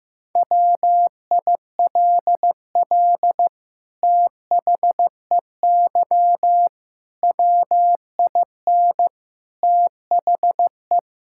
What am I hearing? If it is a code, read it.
Morse code, heard as a keyed tone, about 16 wpm, 700 Hz. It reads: WILL THEY WIN THE
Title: Sentences from Top 1600 Words 15wpm